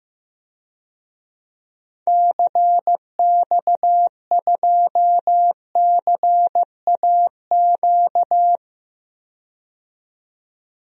Morse code, heard as a keyed tone, about 15 wpm, 700 Hz. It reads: CX2CAQ
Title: DXCC-100 Call Signs - 6 Characters - 15wpm